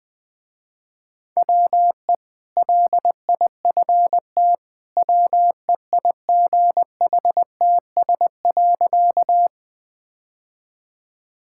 Morse code, heard as a keyed tone, about 20 wpm, 700 Hz. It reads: WE LIFT WEIGHTS.
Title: Easy Sentences with Hints - 20wpm